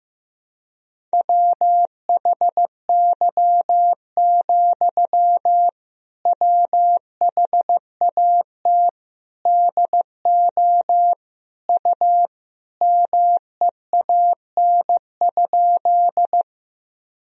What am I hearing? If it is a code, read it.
Morse code, heard as a keyed tone, about 16 wpm, 700 Hz. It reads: WHY, WHAT DO U MEAN?
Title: Sentences from Top 200 Words and Abbreviations 15wpm